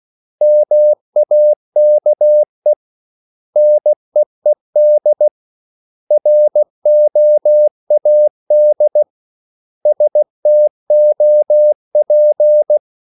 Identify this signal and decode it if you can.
Morse code, heard as a keyed tone, about 16 wpm, 600 Hz. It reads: MAKE NEED ROAD STOP
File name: CW / Practice / 500MostCommonEnglishWords16